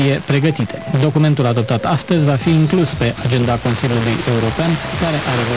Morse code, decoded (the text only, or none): none